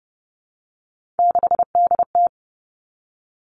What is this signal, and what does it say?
Morse code, heard as a keyed tone, about 30 wpm, 700 Hz. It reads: 6DT
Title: Three Challenging Letter-Number Combinations 30wpm